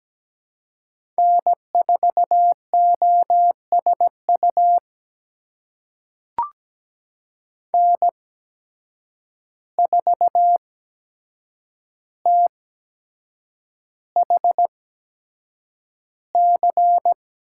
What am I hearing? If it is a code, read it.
Morse code, heard as a keyed tone, about 17 wpm, spaced as for 7 wpm, 700 Hz. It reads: N4OSU N4THC
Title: USA Call Signs - 5 Characters - 8x Character Spacing - 17wpm